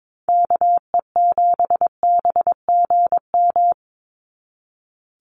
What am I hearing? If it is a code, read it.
Morse code, heard as a keyed tone, about 22 wpm, 700 Hz. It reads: KE7BGM
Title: NA SOTA Activator Callsigns 22wpm